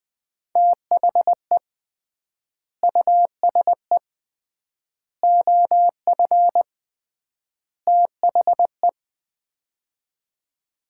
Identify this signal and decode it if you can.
Morse code, heard as a keyed tone, about 20 wpm, 700 Hz. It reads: THE USE OF THE
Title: 3x Word-Spacing - Sets of 4 Words - 20wpm